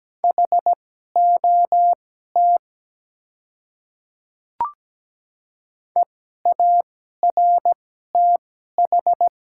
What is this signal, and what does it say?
Morse code, heard as a keyed tone, about 17 wpm, spaced as for 12 wpm, 700 Hz. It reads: HOT EARTH
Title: 2x Character Spacing - Top 200 Words - 17wpm